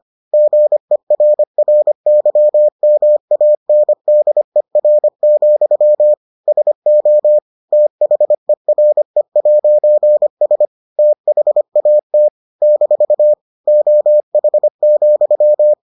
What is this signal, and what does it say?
Morse code, heard as a keyed tone, about 25 wpm, 600 Hz. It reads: GERRYMANDER, SO THERE'S THAT - OH,